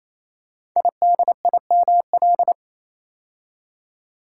Morse code, heard as a keyed tone, about 27 wpm, 700 Hz. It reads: IDIML
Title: Code Groups - 5 Letter:Numbers - 28wpm